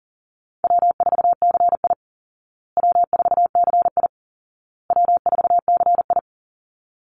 Morse code, heard as a keyed tone, about 38 wpm, 700 Hz. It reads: W4CI W4CI W4CI
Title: Call Signs Repeated 2x - 40wpm